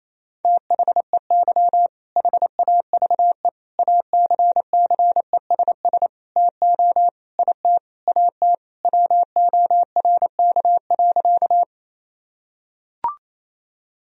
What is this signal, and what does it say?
Morse code, heard as a keyed tone, about 28 wpm, 700 Hz. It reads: THEY HAVE ACCESS TO IT AT WORK.
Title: Sentences from Top 1000 Words - Review Part 4 - 28wpm